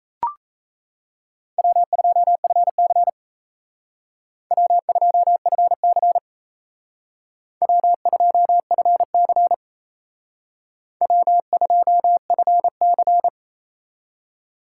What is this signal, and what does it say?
Morse code, heard as a keyed tone, about 36 wpm, 700 Hz. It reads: W2FC W2FC W2FC W2FC